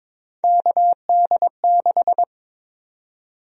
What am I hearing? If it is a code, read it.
Morse code, heard as a keyed tone, about 22 wpm, 700 Hz. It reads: KD6
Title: USA Call Sign Prefixes - 22wpm